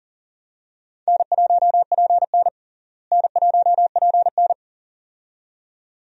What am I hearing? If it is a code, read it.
Morse code, heard as a keyed tone, about 40 wpm, 700 Hz. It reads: N1PN N1PN
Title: USA Call Signs - 4 Characters - Repeated 2x - 40wpm